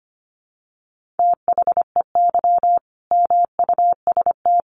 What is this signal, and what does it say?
Morse code, heard as a keyed tone, about 25 wpm, 700 Hz. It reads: THEY MUST
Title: Sentences from Top 1400 Words 25wpm